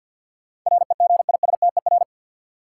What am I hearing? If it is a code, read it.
Morse code, heard as a keyed tone, about 50 wpm, 700 Hz. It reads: REGISTER
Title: Sentences from Top 1200 Words - New Word Drill - 50wpm